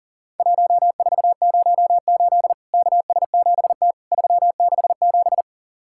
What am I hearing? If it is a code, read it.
Morse code, heard as a keyed tone, about 40 wpm, 700 Hz. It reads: 1408 KS7T 367
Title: Worked All Europe DX Contest - QTC Exchange - 40wpm